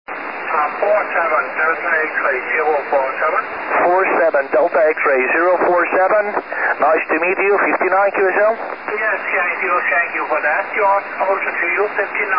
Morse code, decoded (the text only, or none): none